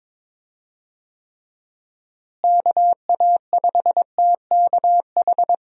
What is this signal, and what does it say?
Morse code, heard as a keyed tone, about 22 wpm, 700 Hz. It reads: KA5TKH
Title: USA Call Signs - 6 Characters - Repeated 2x - 22wpm